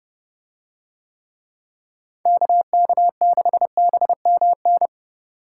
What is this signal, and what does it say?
Morse code, heard as a keyed tone, about 30 wpm, 700 Hz. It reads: KK6BMN